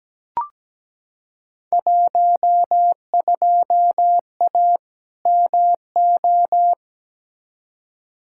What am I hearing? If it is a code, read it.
Morse code, heard as a keyed tone, about 17 wpm, 700 Hz. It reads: 12A MO
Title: Field Day Exchanges Encore - 17wpm